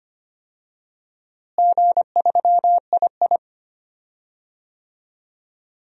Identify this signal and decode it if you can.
Morse code, heard as a keyed tone, about 25 wpm, 700 Hz. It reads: G3II
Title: DXCC-100 Call Signs - 4 Characters - Repeated 2x - 25wpm